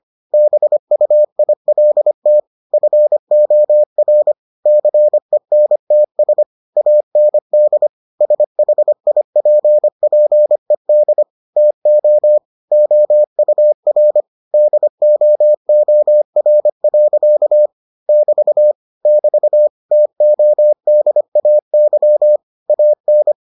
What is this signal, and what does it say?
Morse code, heard as a keyed tone, about 25 wpm, 600 Hz. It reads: BUILT FOR CENTS AND SHIPPED TO OUR DOOR. <BT> <BT> TODAY AN